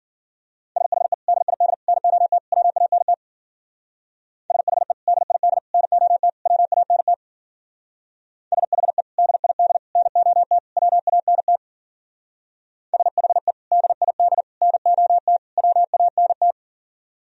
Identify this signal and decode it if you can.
Morse code, heard as a keyed tone, about 47 wpm, 700 Hz. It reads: SHE DID NOT WANT SHE DID NOT WANT SHE DID NOT WANT SHE DID NOT WANT
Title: Speed-Racing - Sets of 4 Words 40wpm